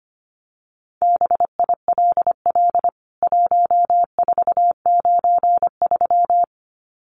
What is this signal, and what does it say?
Morse code, heard as a keyed tone, about 25 wpm, 700 Hz. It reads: BILL 1493
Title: CWT Exchange 25wpm